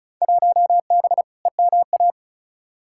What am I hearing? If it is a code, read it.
Morse code, heard as a keyed tone, about 34 wpm, 700 Hz. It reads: 1B EMA
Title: Field Day Exchanges - 35wpm